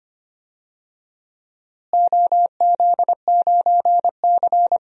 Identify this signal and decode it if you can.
Morse code, heard as a keyed tone, about 25 wpm, 700 Hz. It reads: OZ9C
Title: DXCC-100 Call Signs - 4 Characters - Repeated 2x - 25wpm